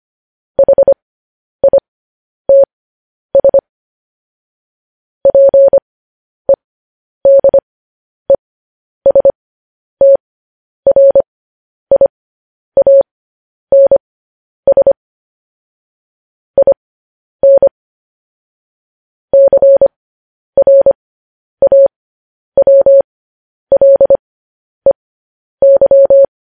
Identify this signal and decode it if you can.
Morse code, heard as a keyed tone, about 25 wpm, spaced as for 10 wpm, 550 Hz. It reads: HITS PEDESTRIANS IN CRAWLEY
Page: News Headlines in Morse Code at 10 WPM